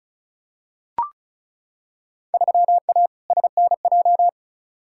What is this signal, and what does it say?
Morse code, heard as a keyed tone, about 36 wpm, 700 Hz. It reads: 3A SNJ
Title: Field Day Exchanges - 35wpm